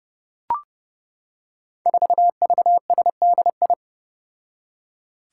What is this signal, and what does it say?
Morse code, heard as a keyed tone, about 30 wpm, 700 Hz. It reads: 4VSDI
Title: Five Challenging Letter-Number Combinations 30wpm